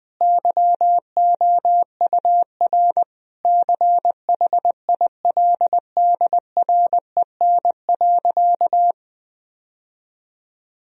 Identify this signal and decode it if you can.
Morse code, heard as a keyed tone, about 20 wpm, 700 Hz. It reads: YOUR CHILDREN.
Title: Sentences from Top 1000 Words - Review Part 4 - 20wpm